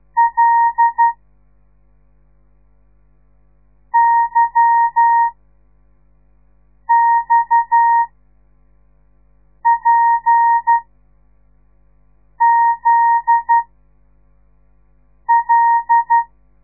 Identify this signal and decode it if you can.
Morse code, heard as a keyed tone, about 12 wpm, 930 Hz. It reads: L Y X P Z L
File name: radiomorningstar / morse / les 8